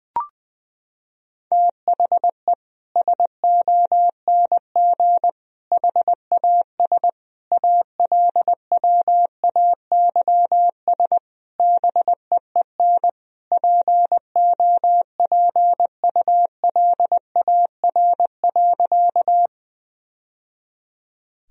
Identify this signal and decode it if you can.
Morse code, heard as a keyed tone, about 21 wpm, 700 Hz. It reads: THE SONG HAS ALWAYS BEEN POPULAR.
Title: Sentences from Top 1000 Words - Review Part 3 - 20wpm